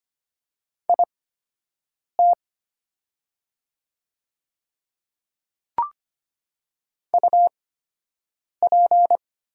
Morse code, heard as a keyed tone, about 25 wpm, spaced as for 7 wpm, 700 Hz. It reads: IT UP
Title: 8x Character Spacing - Top 500 Words - 25wpm